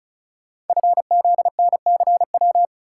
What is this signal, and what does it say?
Morse code, heard as a keyed tone, about 36 wpm, 700 Hz. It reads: FZNCW
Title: Code Groups - 5 Letters - 35wpm